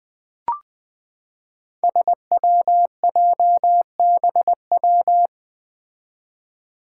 Morse code, heard as a keyed tone, about 21 wpm, 700 Hz. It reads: SWJBW
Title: Five Challenging Letter-Number Combinations 20wpm